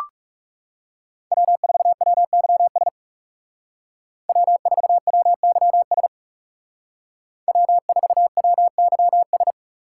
Morse code, heard as a keyed tone, about 36 wpm, 700 Hz. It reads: W4WYS W4WYS W4WYS
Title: Speed Racing - USA 1x3 Call Signs 30wpm